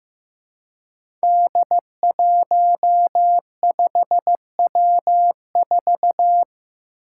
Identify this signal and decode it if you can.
Morse code, heard as a keyed tone, about 15 wpm, 700 Hz. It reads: D15W4
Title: Five Challenging Letter-Number Combinations 15wpm